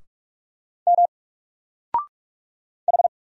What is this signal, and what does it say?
Morse code, heard as a keyed tone, about 44 wpm, 700 Hz. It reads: M H
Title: Single Character (Letter:Number:Punctuation:Prosign) 45wpm